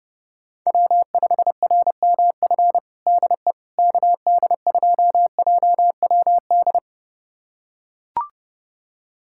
Morse code, heard as a keyed tone, about 32 wpm, 700 Hz. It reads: W5RMF DE KD2JWD
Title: USA Callsign DE Callsign 30wpm